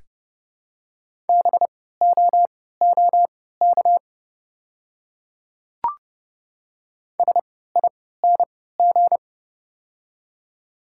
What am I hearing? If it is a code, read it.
Morse code, heard as a keyed tone, about 30 wpm, spaced as for 17 wpm, 700 Hz. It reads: BOOK SING
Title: 3x Character Spacing - Top 500 Words - 30wpm